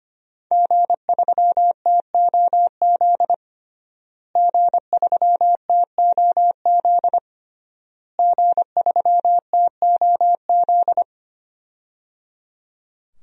Morse code, heard as a keyed tone, about 25 wpm, 700 Hz. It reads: G3TOZ G3TOZ G3TOZ
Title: DXCC-100 Call Signs - 5 Characters - Repeated 3x - 25wpm